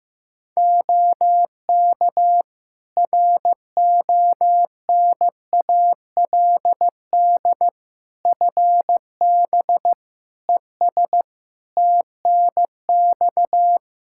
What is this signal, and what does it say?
Morse code, heard as a keyed tone, about 15 wpm, 700 Hz. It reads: OK RONALD FB ES TNX
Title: Practice QSO - Exchange 2 - Info - 15wpm